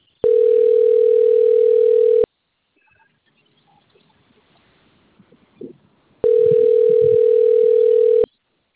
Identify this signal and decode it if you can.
Ringback tone (North America), 440 + 480 Hz.
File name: rg-1600-unknown- - - .4453